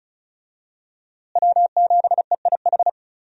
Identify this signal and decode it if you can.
Morse code, heard as a keyed tone, about 34 wpm, 700 Hz. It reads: W7EIH
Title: USA Call Signs - 5 Characters - Repeated 2x - 35wpm